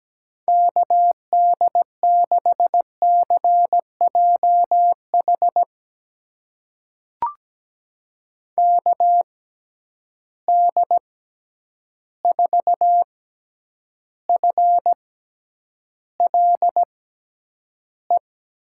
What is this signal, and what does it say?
Morse code, heard as a keyed tone, about 17 wpm, spaced as for 9 wpm, 700 Hz. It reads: KD6CJH KD4FLE
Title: USA Call Signs - 6 Characters - 6x Character Spacing - 17wpm